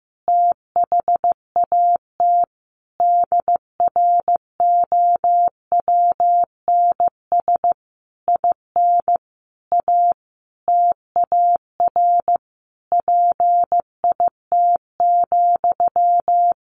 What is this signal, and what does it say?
Morse code, heard as a keyed tone, about 15 wpm, 700 Hz. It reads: THAT DROWNS IN A TAR PIT,
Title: A Sound Of Thunder 15wpm